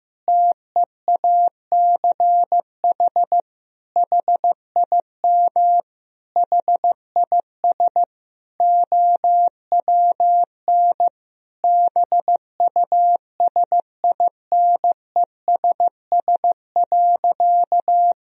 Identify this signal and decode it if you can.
Morse code, heard as a keyed tone, about 15 wpm, 700 Hz. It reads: TEACH HIM HIS OWN BUSINESS.
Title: Sentences from Top 1000 Words - Review Part 2 - 15wpm